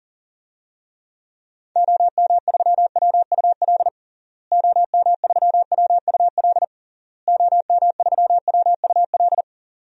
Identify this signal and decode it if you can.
Morse code, heard as a keyed tone, about 40 wpm, 700 Hz. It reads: OM3WUL OM3WUL OM3WUL
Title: DXCC-100 Call Signs - 6 Characters - Repeated 3x - 40wpm